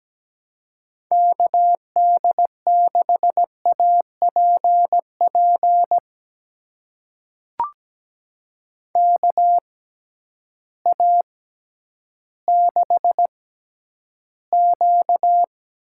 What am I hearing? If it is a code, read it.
Morse code, heard as a keyed tone, about 17 wpm, spaced as for 10 wpm, 700 Hz. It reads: KD6APP KA6Q
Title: USA Call Signs - 6 Characters - 6x Character Spacing - 17wpm